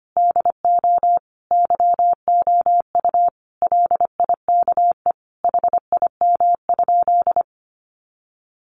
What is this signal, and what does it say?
Morse code, heard as a keyed tone, about 25 wpm, 700 Hz. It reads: DO YOU LIKE HIM?
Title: Sentences from Top 100 Words 25wpm